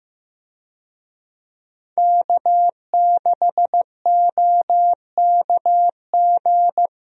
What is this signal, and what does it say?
Morse code, heard as a keyed tone, about 15 wpm, 700 Hz. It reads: K6OKG
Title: USA Call Signs - 5 Characters - 15wpm